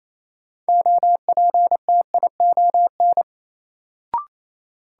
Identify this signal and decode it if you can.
Morse code, heard as a keyed tone, about 28 wpm, 700 Hz. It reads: OPTION
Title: Sentences from Top 1300 Words - New Word Drill - 28wpm